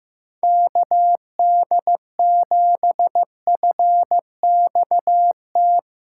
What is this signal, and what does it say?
Morse code, heard as a keyed tone, about 15 wpm, 700 Hz. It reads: KD7FXT
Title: USA Callsign DE Callsign - 6 Characters - 15wpm